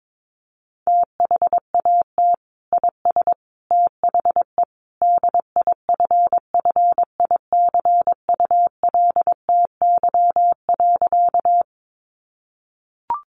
Morse code, heard as a keyed tone, about 22 wpm, 700 Hz. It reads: THAT IS THE DIFFICULTY.
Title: Sentences from Top 1500 Words 22wpm